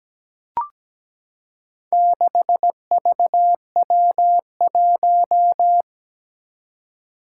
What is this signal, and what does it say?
Morse code, heard as a keyed tone, about 17 wpm, 700 Hz. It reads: 6VW1
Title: Four Challenging Letter-Number Combinations 17wpm